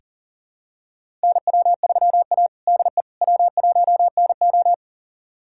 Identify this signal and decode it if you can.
Morse code, heard as a keyed tone, about 40 wpm, 700 Hz. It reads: NW3A DE W1NO
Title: USA Callsign DE Callsign - 4 Characters - 40wpm